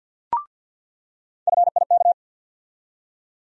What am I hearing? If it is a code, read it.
Morse code, heard as a keyed tone, about 48 wpm, 700 Hz. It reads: FIK
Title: USA Call Sign Postfixes - 3 Letters 50wpm